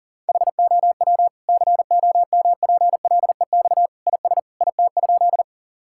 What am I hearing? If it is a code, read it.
Morse code, heard as a keyed tone, about 40 wpm, 700 Hz. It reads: HOW COMPLEX IS IT?
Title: Sentences from Top 1000 Words - Review Part 6 - 40wpm